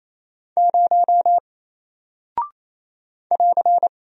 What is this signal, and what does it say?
Morse code, heard as a keyed tone, about 27 wpm, 700 Hz. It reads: 0 <AR>
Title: Single Character (Letter:Number:Punctuation:Prosign) Alt 28wpm